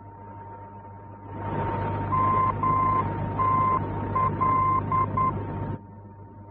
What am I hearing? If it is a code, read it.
Morse code, heard as a keyed tone, about 10 wpm, 1040 Hz, with noise. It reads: MTL
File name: france / mx / MTL-F-VOR